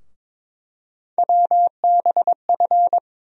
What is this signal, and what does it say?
Morse code, heard as a keyed tone, about 22 wpm, 700 Hz. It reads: WBF
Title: USA Call Sign Postfixes - 3 Letters 22wpm